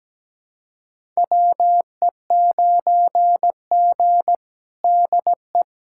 Morse code, heard as a keyed tone, about 17 wpm, 700 Hz. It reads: WE9G DE
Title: USA Callsign DE Callsign - 4 Characters - 17wpm